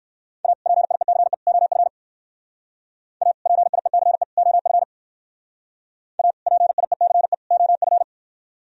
Morse code, heard as a keyed tone, about 58 wpm, 700 Hz. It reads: A PIECE OF A PIECE OF A PIECE OF